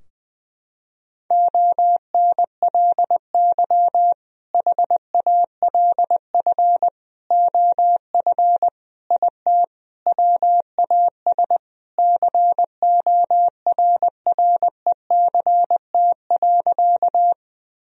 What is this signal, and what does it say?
Morse code, heard as a keyed tone, about 20 wpm, 700 Hz. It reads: ONLY HALF OF IT WAS CORRECT.
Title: Sentences from Top 1000 Words - Review Part 2 - 20wpm